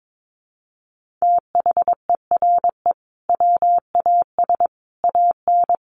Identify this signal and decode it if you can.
Morse code, heard as a keyed tone, about 22 wpm, 700 Hz. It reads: THERE WAS AN
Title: Sentences from Top 1400 Words 22wpm